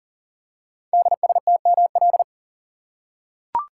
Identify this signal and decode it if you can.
Morse code, heard as a keyed tone, about 39 wpm, 690 Hz. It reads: DSTML
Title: Code Groups - 5 Letters - 40wpm